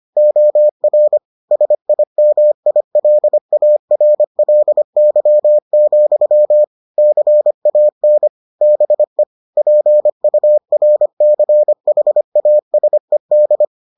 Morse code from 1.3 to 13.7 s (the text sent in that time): SIMILARLY, CAN BE PURCHASED